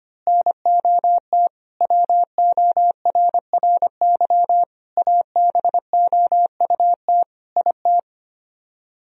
Morse code, heard as a keyed tone, about 25 wpm, 700 Hz. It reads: NOT WORRY ABOUT IT
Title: Speed-Racing - Sets of 5 Words 25wpm